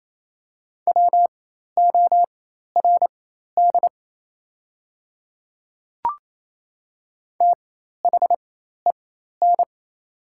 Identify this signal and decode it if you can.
Morse code, heard as a keyed tone, about 28 wpm, spaced as for 13 wpm, 700 Hz. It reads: WORD THEN
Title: 4x Character Spacing - Top 100 Words - 28wpm